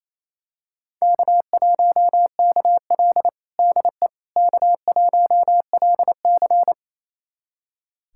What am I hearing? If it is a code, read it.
Morse code, heard as a keyed tone, about 28 wpm, 700 Hz. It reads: K1KL DE K1LC
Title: USA Callsign DE Callsign - 4 Characters - 28wpm